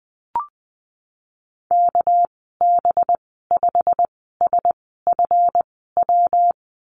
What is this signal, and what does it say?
Morse code, heard as a keyed tone, about 20 wpm, spaced as for 15 wpm, 700 Hz. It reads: KB5SFW
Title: 238 Lesson 38g - Letter q - Call Signs 20wpm